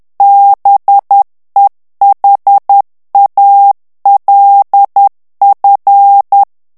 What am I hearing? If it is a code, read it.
Morse code, heard as a keyed tone, about 10 wpm, 800 Hz. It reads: BEHALF
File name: ANGLAIS texte clair 3 vitesse 10 mots minute